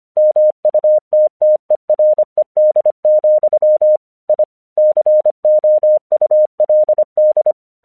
Morse code, heard as a keyed tone, about 25 wpm, 610 Hz. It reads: MUTTERED, I COULD
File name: 1 English 25wpm / AQC24